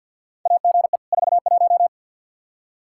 Morse code, heard as a keyed tone, about 53 wpm, 700 Hz. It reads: AGE 41
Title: Practice QSO - Exchange 3 - More Info - 50wpm